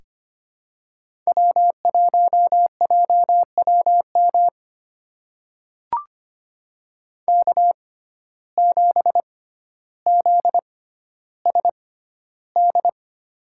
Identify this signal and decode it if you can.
Morse code, heard as a keyed tone, about 25 wpm, spaced as for 13 wpm, 700 Hz. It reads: W1JWM K7ZSD
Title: USA Call Signs - 6x Character Spacing - 25wpm